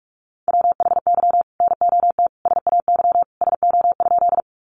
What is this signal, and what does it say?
Morse code, heard as a keyed tone, about 44 wpm, 700 Hz. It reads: WHY NOT SAY SO?
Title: Sentences from Top 600 Words 45wpm